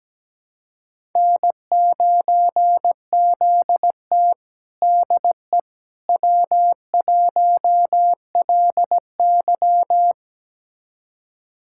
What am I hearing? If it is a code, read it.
Morse code, heard as a keyed tone, about 17 wpm, 700 Hz. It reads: N9ZT DE W1LY
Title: USA Callsign DE Callsign - 4 Characters - 17wpm